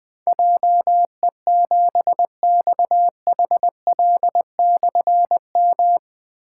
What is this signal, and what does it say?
Morse code, heard as a keyed tone, about 20 wpm, 700 Hz. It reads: JE7XHL/M
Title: Difficult Call Signs 20wpm